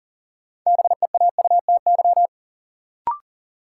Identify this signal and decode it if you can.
Morse code, heard as a keyed tone, about 37 wpm, 700 Hz. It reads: BEAUTY
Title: Sentences from Top 600 Words - New Word Drill - 40wpm